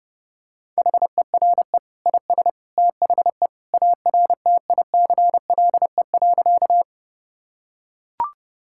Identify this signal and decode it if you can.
Morse code, heard as a keyed tone, about 30 wpm, 700 Hz. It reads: HERE IS THE ARTICLE.
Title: Sentences from Top 1000 Words - Review Part 5 - 30wpm